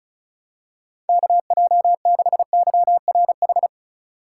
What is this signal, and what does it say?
Morse code, heard as a keyed tone, about 34 wpm, 700 Hz. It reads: KJ6YRH
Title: USA Call Signs - 6 Characters - 4x Character Spacing - 35wpm